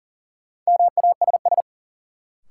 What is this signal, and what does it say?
Morse code, heard as a keyed tone, about 40 wpm, 700 Hz. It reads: MASS
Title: Sets of 5 Words Encore - New Word Drill - 40wpm